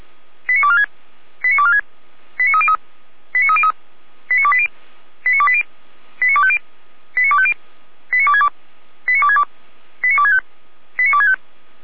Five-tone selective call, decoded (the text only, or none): 89246 89246 89303 89303 89280 89280 89250 89250 89262 89262 89265 89265